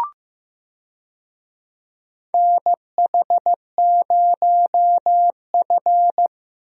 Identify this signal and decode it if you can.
Morse code, heard as a keyed tone, about 15 wpm, 700 Hz. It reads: NH0F
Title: USA Call Signs - 4 Characters - Repeated 2x - 15wpm